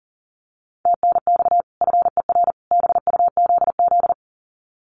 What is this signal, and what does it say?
Morse code, heard as a keyed tone, about 40 wpm, 700 Hz. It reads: TNX FER BUZZ
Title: Common QSO Phrases 40wpm